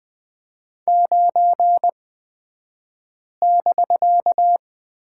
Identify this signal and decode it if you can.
Morse code, heard as a keyed tone, about 20 wpm, spaced as for 5 wpm, 700 Hz. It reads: 9<BK>
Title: 254 Lesson 41d - Period - Two Characters 20wpm